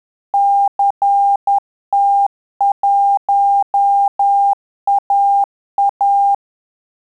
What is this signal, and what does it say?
Morse code, heard as a keyed tone, about 10 wpm, 800 Hz. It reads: CT1AA
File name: Telegraphy-group / QSO CW 50 CPM